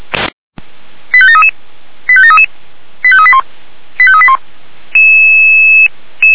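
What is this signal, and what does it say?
Five-tone selective call (ZVEI), heard as quick runs of five tones, tones of about 70 ms. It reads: 85733 85733 85381 85381
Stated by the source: ZVEI five-tone fire brigade call 85751 received on an SDR, logged by Feuerwehr Orlamünde.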